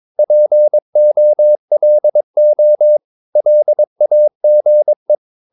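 Morse code, heard as a keyed tone, about 22 wpm, 600 Hz. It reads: POLO LAGE
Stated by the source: Morse practice of short Danish words at 22 wpm, from CW med Gnister.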